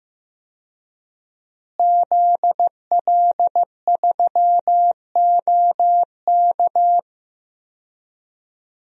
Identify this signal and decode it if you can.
Morse code, heard as a keyed tone, about 15 wpm, 700 Hz. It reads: ZL3OK
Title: DXCC-100 Call Signs - 5 Characters - Repeated 2x - 15wpm